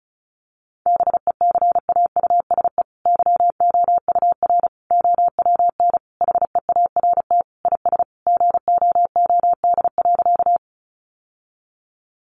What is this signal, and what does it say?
Morse code, heard as a keyed tone, about 34 wpm, 700 Hz. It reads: BECAUSE YOUR OWN HEART IS GOOD.
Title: Sentences from Top 1200 Words 35wpm